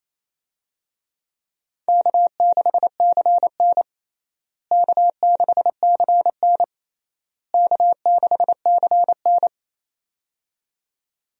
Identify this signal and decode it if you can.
Morse code, heard as a keyed tone, about 28 wpm, 700 Hz. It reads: K6CN K6CN K6CN
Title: USA Call Signs - 4 Characters - Repeated 3x - 28wpm